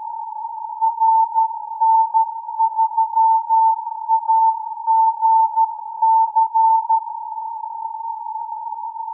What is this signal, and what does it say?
Morse code, heard as a keyed tone, about 13 wpm, 890 Hz. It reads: RN3AGC
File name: CW AGC with filter 23jan 2015z